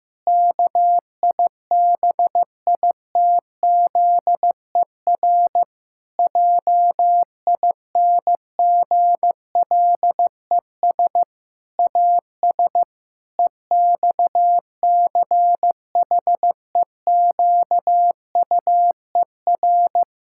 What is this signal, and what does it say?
Morse code, heard as a keyed tone, about 16 wpm, 700 Hz. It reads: KIBITZER JINGLES AS EXCHEQUER
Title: Pangram Challenge v2 - 15wpm